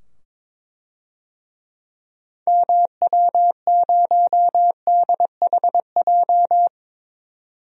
Morse code, heard as a keyed tone, about 21 wpm, 700 Hz. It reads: MW0DHJ
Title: DXCC-100 Call Signs - 6 Characters - 22wpm